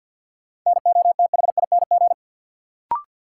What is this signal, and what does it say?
Morse code, heard as a keyed tone, about 53 wpm, 700 Hz. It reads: NOTHING
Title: Sets of 4 Words - New Word Drill - 50wpm